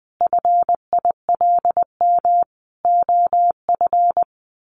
Morse code, heard as a keyed tone, about 20 wpm, 700 Hz. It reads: FILM OF
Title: A Sound Of Thunder 20wpm